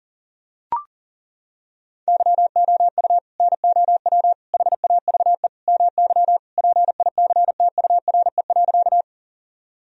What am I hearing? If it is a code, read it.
Morse code, heard as a keyed tone, about 40 wpm, 700 Hz. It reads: YOU NOW HAVE MY PICTURE.
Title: Sentences from Top 1000 Words - Review Part 1 - 40wpm